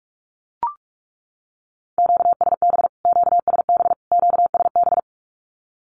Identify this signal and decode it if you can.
Morse code, heard as a keyed tone, about 44 wpm, 700 Hz. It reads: QSB QSB QSB
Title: Q-Codes Repeated 3x 45wpm